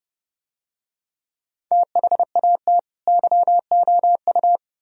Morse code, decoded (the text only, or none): THAT YOU